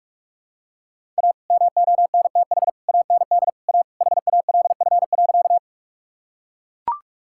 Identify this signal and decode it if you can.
Morse code, heard as a keyed tone, about 41 wpm, 700 Hz. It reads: A MONTH AND A HALF.
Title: Sentences from Top 1000 Words - Review Part 5 - 45wpm